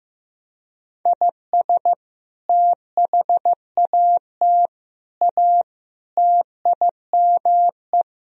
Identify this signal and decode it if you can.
Morse code, heard as a keyed tone, about 16 wpm, 700 Hz. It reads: IS THAT A TIME
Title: Sentences from Top 1000 Words - Review Part 2 - 15wpm